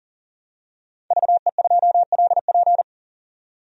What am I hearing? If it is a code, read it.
Morse code, heard as a keyed tone, about 40 wpm, 700 Hz. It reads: VE2LP
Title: DXCC-100 Call Signs - 5 Characters - 40wpm